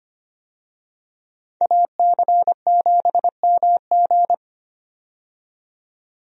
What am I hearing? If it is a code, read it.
Morse code, heard as a keyed tone, about 25 wpm, 700 Hz. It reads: AC7MG
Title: USA Call Signs - 5 Characters - 25wpm